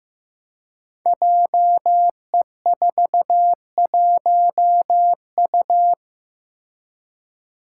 Morse code, heard as a keyed tone, about 15 wpm, 700 Hz. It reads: JE41U
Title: Five Challenging Letter-Number Combinations 15wpm